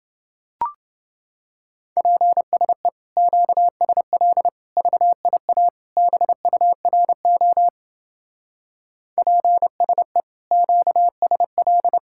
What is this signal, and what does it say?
Morse code, heard as a keyed tone, about 28 wpm, 700 Hz. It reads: PSE QSL VIA BURO PSE QSL